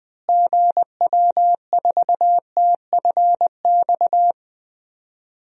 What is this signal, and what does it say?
Morse code, heard as a keyed tone, about 20 wpm, 700 Hz. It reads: GW4TFX
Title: DXCC-100 Call Signs - 6 Characters - Repeated 2x - 20wpm